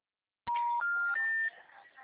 Special information tone.
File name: out- -2018- - - .748